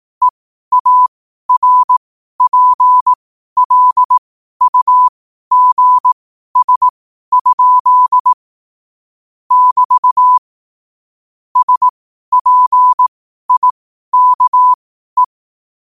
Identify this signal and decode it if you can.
Morse code, heard as a keyed tone, about 18 wpm, spaced as for 12 wpm, 1000 Hz. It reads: EARPLUGS? <BT> SPIKE
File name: qotdfiles / 12wpm